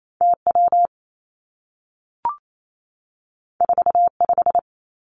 Morse code, heard as a keyed tone, about 28 wpm, 700 Hz. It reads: T W 4 5
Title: Two Challenging Letter-Number Combinations 28wpm